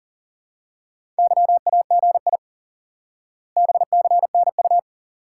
Morse code, heard as a keyed tone, about 40 wpm, 700 Hz. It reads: YAGI BCNU
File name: IQE Course 3 - Lesson 89 - YAGI - Two Elements - 40wpm